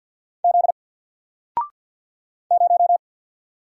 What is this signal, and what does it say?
Morse code, heard as a keyed tone, about 48 wpm, 700 Hz. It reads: Z 0
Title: Single Character (Letter:Number:Punctuation:Prosign) 50wpm